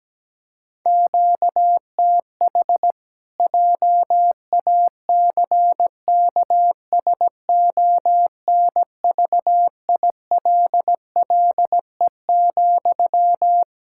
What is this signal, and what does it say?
Morse code, heard as a keyed tone, about 17 wpm, 700 Hz. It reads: QTH JACKSONVILLE,
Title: Practice QSO - Exchange 1 - Beginning - 17wpm